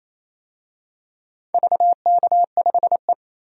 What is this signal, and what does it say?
Morse code, heard as a keyed tone, about 28 wpm, 700 Hz. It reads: VK5E
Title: DXCC-100 Call Signs - 4 Characters - 28wpm